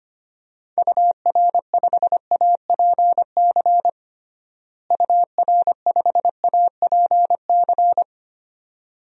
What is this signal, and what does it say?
Morse code, heard as a keyed tone, about 25 wpm, 700 Hz. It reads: UR5APC UR5APC
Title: DXCC-100 Call Signs - 6 Characters - Repeated 2x - 25wpm